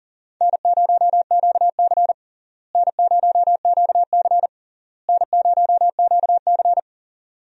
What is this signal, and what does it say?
Morse code, heard as a keyed tone, about 37 wpm, 700 Hz. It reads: N0QC N0QC N0QC
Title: USA Call Signs - 4 Characters - Repeated 3x - 40wpm